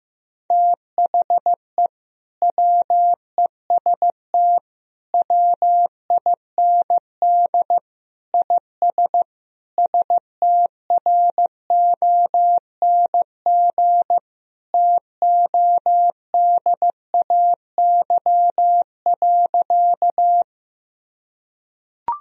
Morse code, heard as a keyed tone, about 15 wpm, 700 Hz. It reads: THE WEST WIND IS STRONG TODAY.
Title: Sentences from Top 1000 Words - Review Part 2 - 15wpm